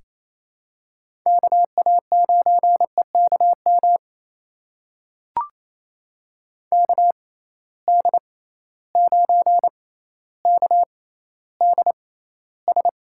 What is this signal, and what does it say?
Morse code, heard as a keyed tone, about 28 wpm, spaced as for 14 wpm, 700 Hz. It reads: KA9EKM KD9KDS
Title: USA Call Signs - 6 Characters - 6x Character Spacing - 28wpm